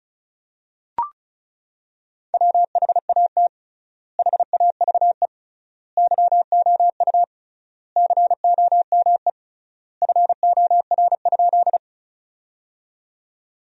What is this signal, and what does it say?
Morse code, heard as a keyed tone, about 34 wpm, 700 Hz. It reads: WHAT HAVE YOU COME FOR?
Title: 3x Word-Spacing - Sentences from Top 100 Words - 35wpm